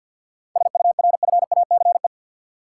Morse code, heard as a keyed tone, about 52 wpm, 700 Hz. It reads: SURFACE
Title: Sets of 4 Words - New Word Drill - 50wpm